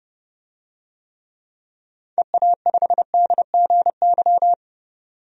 Morse code, heard as a keyed tone, about 32 wpm, 700 Hz. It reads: EA5DGY
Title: DXCC-100 Call Signs - 6 Characters - 30wpm